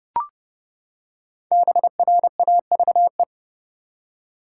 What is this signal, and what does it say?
Morse code, heard as a keyed tone, about 32 wpm, 700 Hz. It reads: BRAVE
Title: Five Letter Words - 30wpm